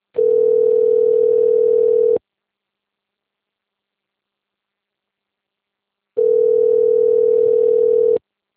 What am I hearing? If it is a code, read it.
Ringback tone (North America), 440 + 480 Hz.
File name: rg-1600-unknown- - - .37203